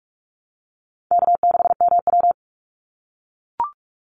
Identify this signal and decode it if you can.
Morse code, heard as a keyed tone, about 44 wpm, 700 Hz. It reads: K6MW
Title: NA SOTA Chaser Callsigns 45wpm